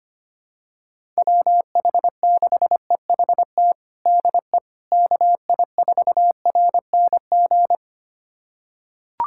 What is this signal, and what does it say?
Morse code, heard as a keyed tone, about 25 wpm, 700 Hz. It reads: WH6EHT DE KI4RNG
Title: USA Callsign DE Callsign 25wpm